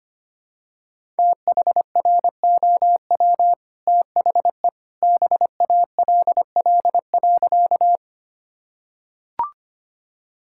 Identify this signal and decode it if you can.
Morse code, heard as a keyed tone, about 25 wpm, 700 Hz. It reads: THROW THE BALL.
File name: Easy Sentences with Hints - 25wpm